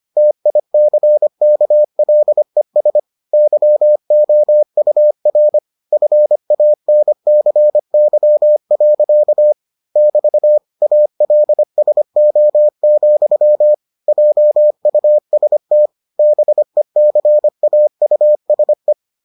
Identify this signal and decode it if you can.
Morse code, heard as a keyed tone, about 25 wpm, 600 Hz. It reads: TICKLES YOUR FANCY. <BT> ALSO, JUST BECAUSE